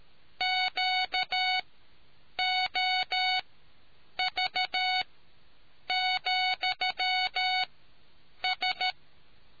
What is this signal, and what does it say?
Morse code, heard as a keyed tone, about 13 wpm, spaced as for 7 wpm, 740 Hz. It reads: QOV,S